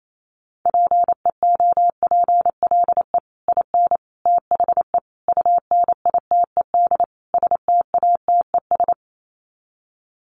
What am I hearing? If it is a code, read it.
Morse code, heard as a keyed tone, about 28 wpm, 700 Hz. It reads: PEOPLE IN THE UNITED STATES
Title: Sets of 5 Words Encore 28wpm